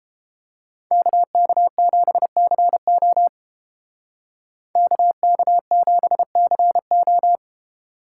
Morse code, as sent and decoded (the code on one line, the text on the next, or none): -.- -.- --... -.-. --- / -.- -.- --... -.-. ---
KK7CO KK7CO